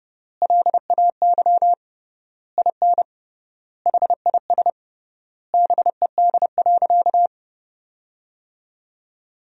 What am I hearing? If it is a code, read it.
Morse code, heard as a keyed tone, about 30 wpm, 700 Hz. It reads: LAY IN HIS BED.
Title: 3x Word-Spacing - Sentences from Top 500 Words - 30wpm